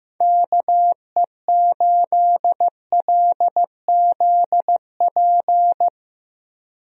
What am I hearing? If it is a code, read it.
Morse code, heard as a keyed tone, about 15 wpm, 700 Hz. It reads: KE8LZP
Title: USA Call Signs - 6 Characters - 15wpm